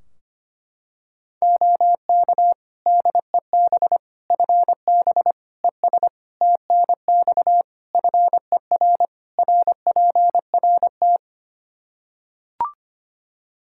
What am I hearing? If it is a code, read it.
Morse code, heard as a keyed tone, about 25 wpm, 700 Hz. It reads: OK DEB FB ES TNX FER RPRT
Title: Practice QSO - Exchange 2 - Info - 25wpm